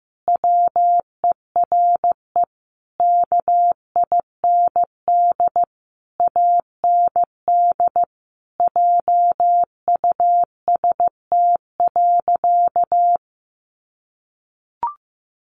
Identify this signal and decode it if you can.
Morse code, heard as a keyed tone, about 14 wpm, 700 Hz. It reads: WERE KIND AND JUST.
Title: Sentences from Top 800 Words 15wpm